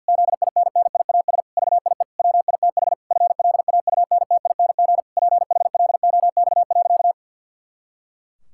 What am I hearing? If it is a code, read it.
Morse code, heard as a keyed tone, about 50 wpm, 700 Hz. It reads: ZINNIAS VIE WITH FLAUNTING PHLOX.
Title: Pangram Challenge v2 - 50wpm